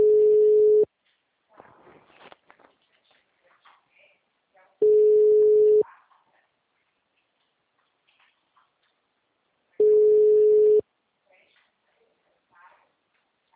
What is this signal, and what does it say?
Telephone call progress tone (Europe, 425 Hz), heard as a telephone network's tone, about 420 Hz, with speech. Ringback tone.